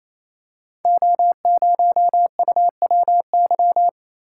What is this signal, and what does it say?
Morse code, heard as a keyed tone, about 28 wpm, 700 Hz. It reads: O0UWY
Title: Code Groups - 5 Letter:Numbers - 28wpm